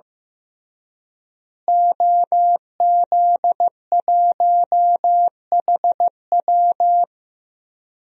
Morse code, heard as a keyed tone, about 15 wpm, 700 Hz. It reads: OZ1HW
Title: DXCC-100 Call Signs - 5 Characters - Repeated 3x - 15wpm